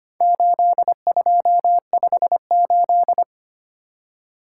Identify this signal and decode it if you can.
Morse code, heard as a keyed tone, about 25 wpm, 700 Hz. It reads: 8258
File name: practice / 25wpm / Four Letter-Number Combinations - 25wpm